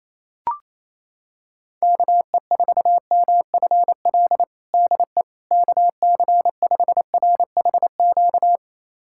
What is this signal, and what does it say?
Morse code, heard as a keyed tone, about 27 wpm, 700 Hz. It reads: KE4MFL DE KC5RHQ
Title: USA Callsign DE Callsign 28wpm